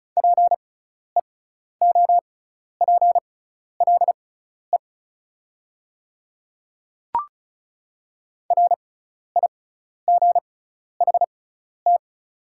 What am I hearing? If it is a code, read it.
Morse code, heard as a keyed tone, about 36 wpm, spaced as for 12 wpm, 690 Hz. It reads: PEOPLE RIGHT
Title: 6x Character Spacing - Top 200 Words - 35wpm